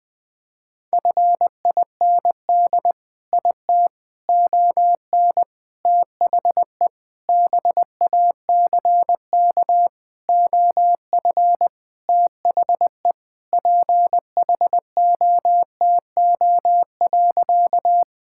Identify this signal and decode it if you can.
Morse code, heard as a keyed tone, about 20 wpm, 700 Hz. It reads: FIND IT ON THE BACK OF THE PHOTO.
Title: Sentences with 4-Word Phrases - Spoken Beginning - 20wpm